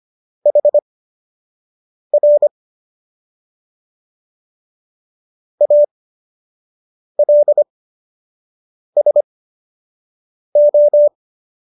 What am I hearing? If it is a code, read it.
Morse code, heard as a keyed tone, about 25 wpm, spaced as for 6 wpm, 590 Hz. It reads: HR ALSO